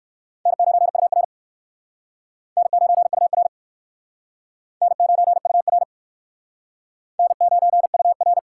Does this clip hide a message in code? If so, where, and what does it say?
Yes: Morse code from 0.5 to 8.4 s, heard as a keyed tone, about 59 wpm, 700 Hz. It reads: N9UR N9UR N9UR N9UR